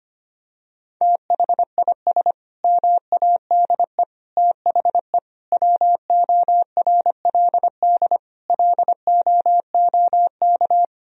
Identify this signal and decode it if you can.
Morse code, heard as a keyed tone, about 25 wpm, 700 Hz. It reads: THIS MADE THE WORLD LOOK
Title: Sentences from Top 1000 Words - Review Part 2 - 25wpm